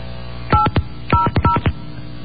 DTMF, heard as telephone touch tones, tones of about 130 ms. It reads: 800
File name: mp3files / off the hook